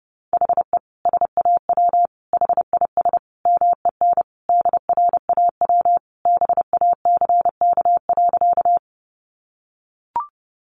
Morse code, heard as a keyed tone, about 30 wpm, 700 Hz. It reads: HE SAW HIS MEN DRAW BACK.
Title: Sentences from Top 300 Words 30wpm